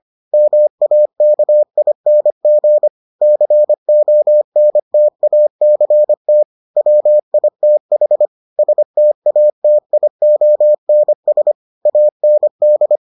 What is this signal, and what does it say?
Morse code, heard as a keyed tone, about 25 wpm, 600 Hz. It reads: MAKING CONTACT WITH STATIONS AND